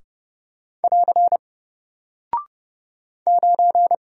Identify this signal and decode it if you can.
Morse code, heard as a keyed tone, about 30 wpm, 700 Hz. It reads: <AR> 9
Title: Single Character (Letter:Number:Punctuation:Prosign) 30wpm